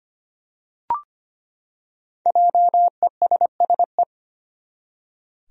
Morse code, heard as a keyed tone, about 25 wpm, 700 Hz. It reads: JESSE
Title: USA Common Names - 25wpm